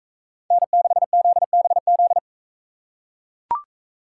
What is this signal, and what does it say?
Morse code, heard as a keyed tone, about 41 wpm, 700 Hz. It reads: N6ZBZ